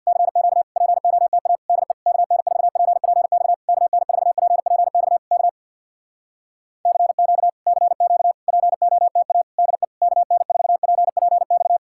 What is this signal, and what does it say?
Morse code, heard as a keyed tone, about 67 wpm, 700 Hz. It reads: CQ POTA DE KN4PPX KN4PPX K CQ CQ POTA DE KN4PPX